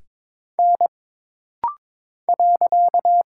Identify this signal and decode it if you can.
Morse code, heard as a keyed tone, about 22 wpm, 700 Hz. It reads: N .
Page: This rapid-fire timing is geared to push your ICR (Instant Character Recognition) skills.